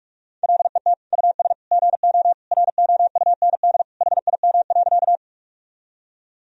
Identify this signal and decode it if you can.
Morse code, heard as a keyed tone, about 43 wpm, 700 Hz. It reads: LET US GO ROUND HIM.
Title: Sentences from Top 1000 Words - Review Part 1 - 45wpm